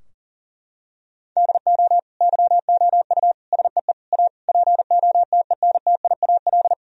Morse code, heard as a keyed tone, about 40 wpm, 700 Hz. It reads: DO YOU SEE A POTENTIAL
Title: Sentences from Top 1000 Words - Review Part 4 - 40wpm